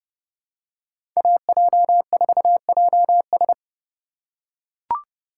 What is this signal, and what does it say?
Morse code, heard as a keyed tone, about 30 wpm, 700 Hz. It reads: AJ4JS
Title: USA Call Signs - 5 Characters - 2x Character Spacing - 30wpm